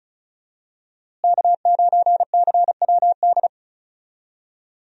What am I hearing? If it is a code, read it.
Morse code, heard as a keyed tone, about 34 wpm, 700 Hz. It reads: K9CWD
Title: USA Call Signs - 5 Characters - Repeated 2x - 35wpm